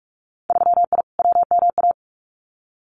Morse code, heard as a keyed tone, about 46 wpm, 700 Hz. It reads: 3I WMA
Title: Winter Field Day Exchanges 45wpm